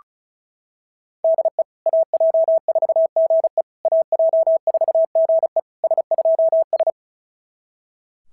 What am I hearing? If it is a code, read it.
Morse code, heard as a keyed tone, about 36 wpm, 690 Hz. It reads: DE AJ4GE AJ4GE S2S
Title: SOTA QSO Practice 35wpm